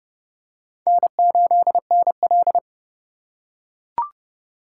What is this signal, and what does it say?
Morse code, heard as a keyed tone, about 28 wpm, 700 Hz. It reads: N8NL